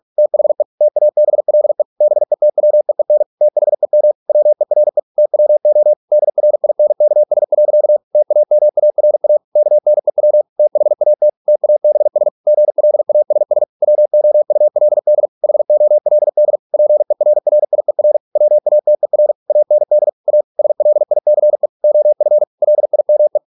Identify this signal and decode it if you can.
Morse code, heard as a keyed tone, about 46 wpm, 600 Hz. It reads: THE TABLE BETWEEN THEM WERE TWO DRINKS. TAMARA KNEW THAT TABS GLASS WOULD HOLD PERRIER WATER AND A SLICE OF LIME